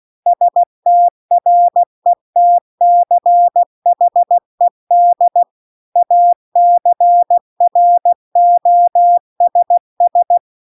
Morse code, heard as a keyed tone, about 16 wpm, 700 Hz. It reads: STRETCHED ACROSS
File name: 16WPM